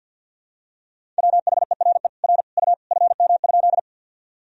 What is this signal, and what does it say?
Morse code, heard as a keyed tone, about 49 wpm, 700 Hz. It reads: WHERE R U FM?
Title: Sentences from Top 200 Words and Abbreviations 50wpm